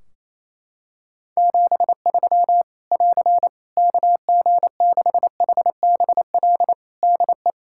Morse code, heard as a keyed tone, about 28 wpm, 700 Hz. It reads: 73 <AR> KG6HBL DE
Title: Practice QSO - Exchange 4 - Ending - 28wpm